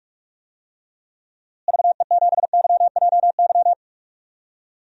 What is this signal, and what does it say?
Morse code, heard as a keyed tone, about 41 wpm, 700 Hz. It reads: VE7YJY
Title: DXCC-100 Call Signs - 6 Characters - Repeated 2x - 45wpm